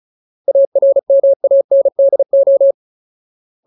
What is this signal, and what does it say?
Morse code, heard as a keyed tone, about 34 wpm, 550 Hz. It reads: ARMANDO
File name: Ultimate Word List / 35WPM Ultimate